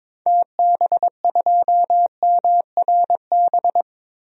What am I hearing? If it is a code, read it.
Morse code, heard as a keyed tone, about 22 wpm, 700 Hz. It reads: TB2MRB
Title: DXCC-100 Call Signs - 6 Characters - Repeated 2x - 22wpm